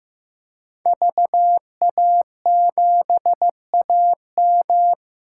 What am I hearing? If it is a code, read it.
Morse code, heard as a keyed tone, about 15 wpm, 700 Hz. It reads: VA7AM
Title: K1USN SST QSO Practice - 15wpm